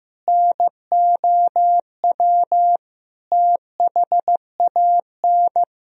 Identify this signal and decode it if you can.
Morse code, heard as a keyed tone, about 15 wpm, 700 Hz. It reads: NOW THAN
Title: Sentences from Top 1000 Words - Review Part 4 - 15wpm